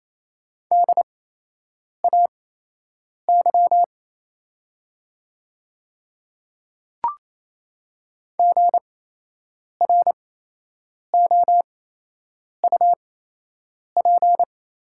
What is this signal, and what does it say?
Morse code, heard as a keyed tone, about 28 wpm, spaced as for 8 wpm, 700 Hz. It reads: DAY GROUP
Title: 8x Character Spacing - Top 300 Words - 28wpm